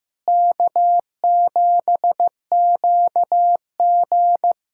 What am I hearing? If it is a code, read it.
Morse code, heard as a keyed tone, about 15 wpm, 700 Hz. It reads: K7QG
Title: USA Callsign DE Callsign - 4 Characters - 15wpm